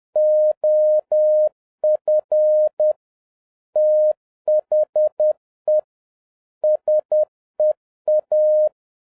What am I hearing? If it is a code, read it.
Morse code, heard as a keyed tone, about 10 wpm, 610 Hz. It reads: OF THE SEA
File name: 2011-07-mp3 / n31 THE 3.11 2 10wpm